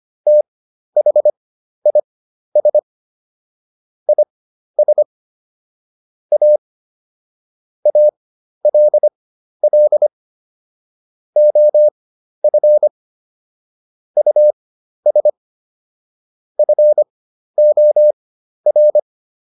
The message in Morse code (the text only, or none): THIS IS A ALL OF US FOR